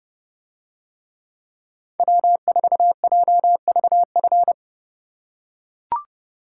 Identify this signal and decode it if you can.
Morse code, heard as a keyed tone, about 29 wpm, 700 Hz. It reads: W4JVF